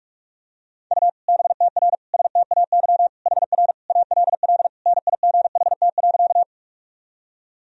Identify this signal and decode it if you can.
Morse code, heard as a keyed tone, about 46 wpm, 700 Hz. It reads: U BTR STAY HR ALL NIGHT.
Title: Sentences from Top 500 Words and Abbreviations 45wpm